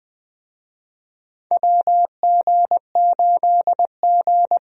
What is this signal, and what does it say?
Morse code, heard as a keyed tone, about 21 wpm, 700 Hz. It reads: WG8G
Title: USA Call Signs - 4 Characters - Repeated 2x - 20wpm